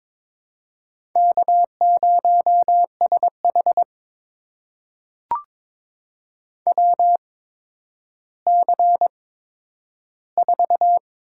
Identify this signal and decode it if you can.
Morse code, heard as a keyed tone, about 22 wpm, spaced as for 11 wpm, 700 Hz. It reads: K0SH WC4
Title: USA Call Signs - 4 Characters - 8x Character Spacing - 22wpm